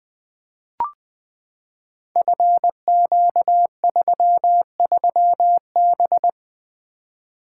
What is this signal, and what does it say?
Morse code, heard as a keyed tone, about 20 wpm, 700 Hz. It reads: FQ33B
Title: Code Groups - 5 Letter:Numbers - 20wpm